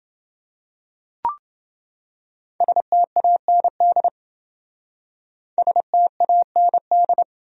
Morse code, heard as a keyed tone, about 27 wpm, 700 Hz. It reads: STAND STAND